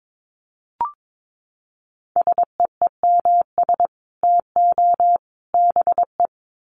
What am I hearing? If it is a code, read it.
Morse code, heard as a keyed tone, about 22 wpm, 700 Hz. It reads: SEEMS TO BE
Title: Sets of 3 Words 22wpm